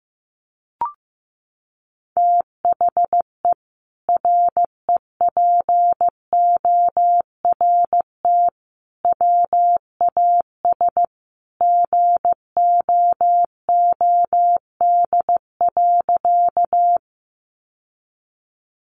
Morse code, heard as a keyed tone, about 15 wpm, 700 Hz. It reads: THE REPORT WAS GOOD.
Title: Sentences from Top 900 Words 15wpm